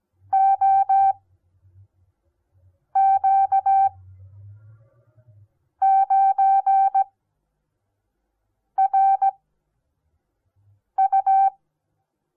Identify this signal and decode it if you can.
Morse code, heard as a keyed tone, about 17 wpm, spaced as for 4 wpm, 780 Hz. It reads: OQ9RU
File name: Onlinelehrgang / morsen / DJ4UF Morsen08